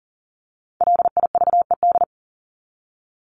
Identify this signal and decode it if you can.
Morse code, heard as a keyed tone, about 40 wpm, 700 Hz. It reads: LIVED
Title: ED Suffix - Top 100 Words with Base Words 40wpm